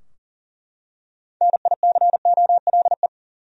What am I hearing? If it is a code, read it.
Morse code, heard as a keyed tone, about 40 wpm, 700 Hz. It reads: NICOLE
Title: USA Common Names - 40wpm